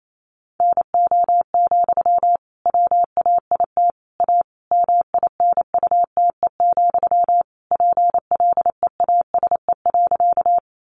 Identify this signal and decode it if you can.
Morse code, heard as a keyed tone, about 28 wpm, 700 Hz. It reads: NO, WAIT A MINUTE, PLEASE.
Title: Sentences from Top 1500 Words 28wpm